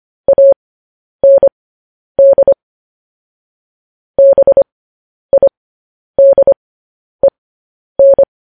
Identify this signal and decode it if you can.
Morse code, heard as a keyed tone, about 25 wpm, spaced as for 10 wpm, 550 Hz. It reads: AND BIDEN